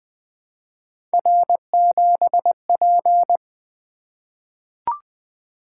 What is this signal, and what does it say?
Morse code, heard as a keyed tone, about 20 wpm, 700 Hz. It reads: R7P
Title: Three Letter-Number Combinations - 20wpm